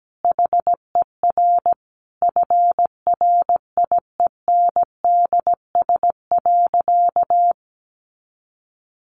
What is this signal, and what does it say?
Morse code, heard as a keyed tone, about 17 wpm, 700 Hz. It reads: HER FRIENDS.
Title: Easy Phrases 17wpm